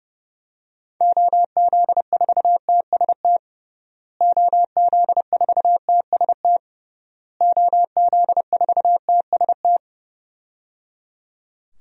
Morse code, heard as a keyed tone, about 30 wpm, 700 Hz. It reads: OZ4TST OZ4TST OZ4TST
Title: DXCC-100 Call Signs - 6 Characters - Repeated 3x - 30wpm